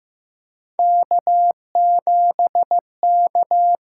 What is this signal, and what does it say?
Morse code, heard as a keyed tone, about 15 wpm, 700 Hz. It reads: K7K
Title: USA Call Signs - 6x Character Spacing - 15wpm